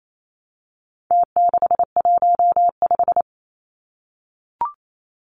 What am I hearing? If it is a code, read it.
Morse code, heard as a keyed tone, about 28 wpm, 700 Hz. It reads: T615
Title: Numbers - Contest Style - 4-Digit 28wpm